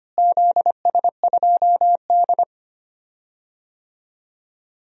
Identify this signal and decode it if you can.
Morse code, heard as a keyed tone, about 25 wpm, 700 Hz. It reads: ZS2D
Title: DXCC-100 Call Signs - 4 Characters - Repeated 2x - 25wpm